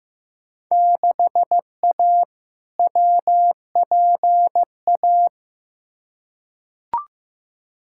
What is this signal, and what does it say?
Morse code, heard as a keyed tone, about 15 wpm, 700 Hz. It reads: 6A WPA
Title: Field Day Exchanges Encore - 15wpm